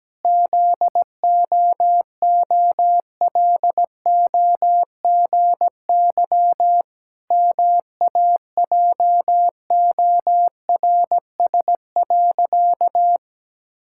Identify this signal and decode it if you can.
Morse code, heard as a keyed tone, about 17 wpm, 700 Hz. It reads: ZOOLOGY MAJORS.
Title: Pangram Challenge v2 - 17wpm